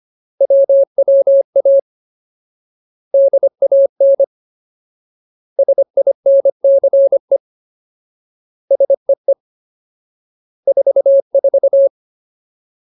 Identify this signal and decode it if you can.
Morse code, heard as a keyed tone, about 25 wpm, 550 Hz. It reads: WWA DAN SINCE SEE 44
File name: Ultimate Word List / 25WPM Ultimate